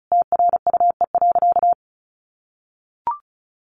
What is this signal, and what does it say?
Morse code, heard as a keyed tone, about 34 wpm, 700 Hz. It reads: TRUE.
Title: Sentences from Top 400 Words 35wpm